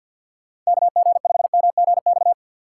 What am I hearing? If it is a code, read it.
Morse code, heard as a keyed tone, about 52 wpm, 700 Hz. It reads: KG5MGX
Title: USA Call Signs - 6 Characters - 4x Character Spacing - 50wpm